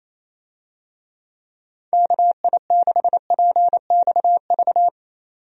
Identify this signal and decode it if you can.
Morse code, heard as a keyed tone, about 28 wpm, 700 Hz. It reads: KI6PXV
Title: USA Call Signs - 6 Characters - 28wpm